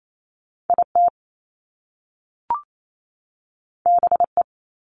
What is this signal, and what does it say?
Morse code, heard as a keyed tone, about 28 wpm, 700 Hz. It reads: IT BE
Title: Top Two Letter Combinations in Words 28wpm